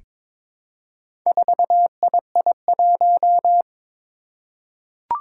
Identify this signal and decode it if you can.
Morse code, heard as a keyed tone, about 22 wpm, 700 Hz. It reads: 4II1
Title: Four Challenging Letter-Number Combinations 22wpm